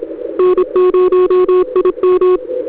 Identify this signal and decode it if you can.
Morse code, heard as a keyed tone, about 27 wpm, 370 Hz. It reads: N0IM
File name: Demo / ReceivedAudio